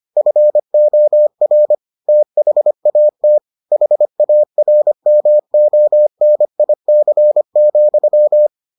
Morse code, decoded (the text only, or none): FOR THAT HARMONIC,